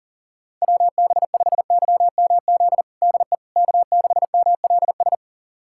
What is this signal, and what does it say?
Morse code, heard as a keyed tone, about 40 wpm, 700 Hz. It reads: WB5YMZ DE K6MLS
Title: USA Callsign DE Callsign 40wpm